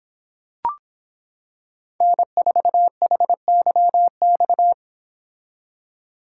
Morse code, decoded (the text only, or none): N4HYX